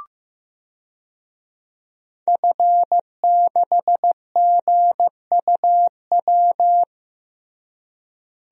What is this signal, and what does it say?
Morse code, heard as a keyed tone, about 15 wpm, 700 Hz. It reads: F6GUW
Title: DXCC-100 Call Signs - 5 Characters - Repeated 2x - 15wpm